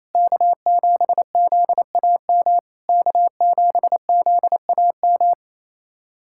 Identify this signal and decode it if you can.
Morse code, heard as a keyed tone, about 28 wpm, 700 Hz. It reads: K7ZAM K7ZAM
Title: SOTA QSO Practice 28wpm